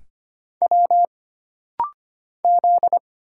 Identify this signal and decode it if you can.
Morse code, heard as a keyed tone, about 25 wpm, 700 Hz. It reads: W Z
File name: Single Character (Letter:Number:Punctuation:Prosign) - Rapid-Fire - 25wpm